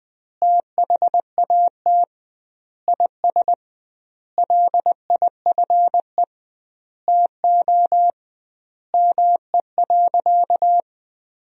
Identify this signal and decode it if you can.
Morse code, heard as a keyed tone, about 21 wpm, 700 Hz. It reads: THAT IS LIFE TO ME.
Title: 2x Word-Spacing - Sentences from Top 500 Words - 20wpm